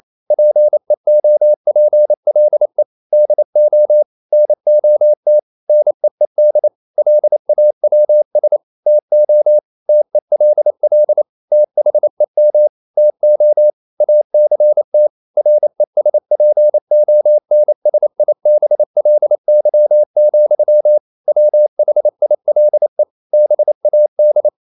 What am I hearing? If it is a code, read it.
Morse code, heard as a keyed tone, about 28 wpm, 600 Hz. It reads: PEOPLE DO NOT NEED LAWS TO TELL THEM TO ACT RESPONSIBLY, WHILE BAD
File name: CW / Practice / EnglishQuotations0128